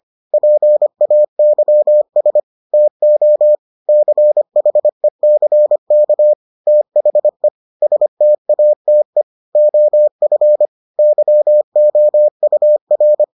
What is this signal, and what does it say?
Morse code, heard as a keyed tone, about 25 wpm, 600 Hz. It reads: PAYS TO CHECK THE STATE OF YOUR